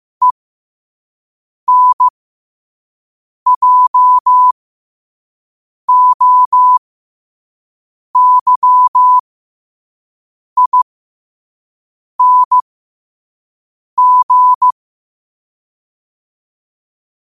Morse code, heard as a keyed tone, about 15 wpm, spaced as for 5 wpm, 1000 Hz. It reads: ENJOYING
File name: qotdfiles / 5wpm